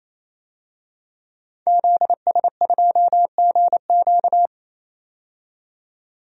Morse code, heard as a keyed tone, about 28 wpm, 700 Hz. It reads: ZS2GQ
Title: DXCC-100 Call Signs - 5 Characters - 28wpm